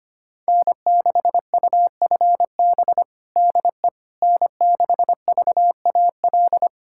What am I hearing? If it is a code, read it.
Morse code, heard as a keyed tone, about 25 wpm, 700 Hz. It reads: N6UFB DE N6VAL
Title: USA Callsign DE Callsign - 5 Characters - 25wpm